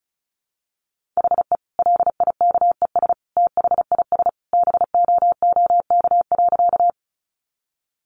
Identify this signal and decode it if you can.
Morse code, heard as a keyed tone, about 34 wpm, 700 Hz. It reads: HE LIKES THIS BOOK.
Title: Additional Easy Phrases 35wpm